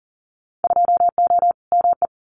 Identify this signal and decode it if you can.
Morse code, heard as a keyed tone, about 40 wpm, 700 Hz. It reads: 2O ME
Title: Winter Field Day Exchanges 40wpm